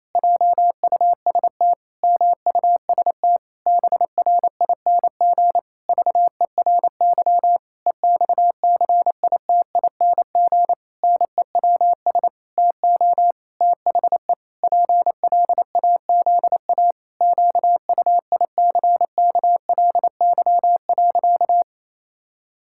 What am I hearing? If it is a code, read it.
Morse code, heard as a keyed tone, about 28 wpm, 700 Hz. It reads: JUST MUST BRING VERY EXCITING NEWS TO THE PLAZA QUICKLY.
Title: Pangram Challenge v2 - 28wpm